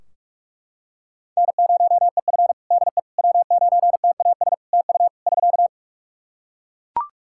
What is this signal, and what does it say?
Morse code, heard as a keyed tone, about 41 wpm, 700 Hz. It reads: N0EF DE W9TAS TU <SK>
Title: Practice QSO - Exchange 4 - Ending - 45wpm